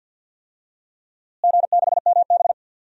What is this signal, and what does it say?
Morse code, heard as a keyed tone, about 53 wpm, 700 Hz. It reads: G6MB
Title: DXCC-100 Call Signs - 4 Characters - Repeated 2x - 50wpm